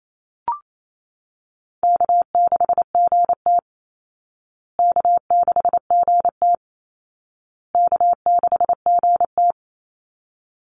Morse code, heard as a keyed tone, about 28 wpm, 700 Hz. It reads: K6GT K6GT K6GT
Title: Call Signs Repeated 2x - 28wpm